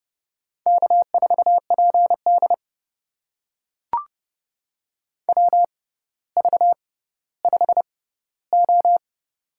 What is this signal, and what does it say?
Morse code, heard as a keyed tone, about 29 wpm, spaced as for 15 wpm, 700 Hz. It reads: K4PD WV5O
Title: USA Call Signs - 4 Characters - 6x Character Spacing - 30wpm